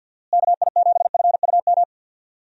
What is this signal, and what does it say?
Morse code, heard as a keyed tone, about 50 wpm, 700 Hz. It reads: KI7FUM
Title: USA Call Signs - 6 Characters - Repeated 2x - 50wpm